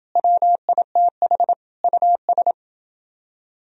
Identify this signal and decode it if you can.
Morse code, heard as a keyed tone, about 27 wpm, 700 Hz. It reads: WITH US